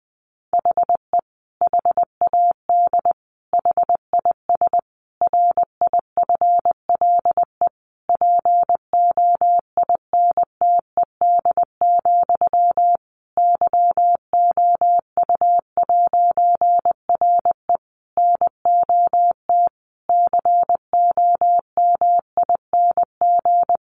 Morse code, heard as a keyed tone, about 20 wpm, 700 Hz. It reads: HE HAD HIS RIFLE POINTED, YOU'RE NOT COMING
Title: A Sound Of Thunder 20wpm